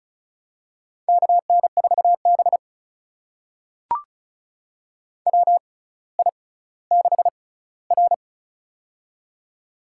Morse code, heard as a keyed tone, about 36 wpm, spaced as for 18 wpm, 700 Hz. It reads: KN4B WI6R
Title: USA Call Signs - 4 Characters - 6x Character Spacing - 35wpm